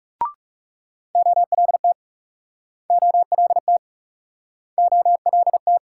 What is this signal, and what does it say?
Morse code, heard as a keyed tone, about 38 wpm, 700 Hz. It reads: OLT OLT OLT